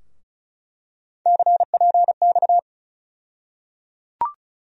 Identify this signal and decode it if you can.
Morse code, heard as a keyed tone, about 36 wpm, 700 Hz. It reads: CPX
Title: USA Call Sign Postfixes - 3 Letters 35wpm